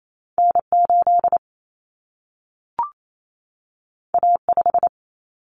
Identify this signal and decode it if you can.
Morse code, heard as a keyed tone, about 28 wpm, 700 Hz. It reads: N8 A5
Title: Common Cut Numbers - 2-Digit 28wpm